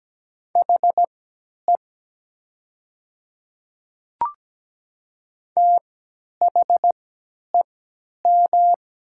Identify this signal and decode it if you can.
Morse code, heard as a keyed tone, about 17 wpm, spaced as for 10 wpm, 700 Hz. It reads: HE THEM
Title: 3x Character Spacing - Top 100 Words - 17wpm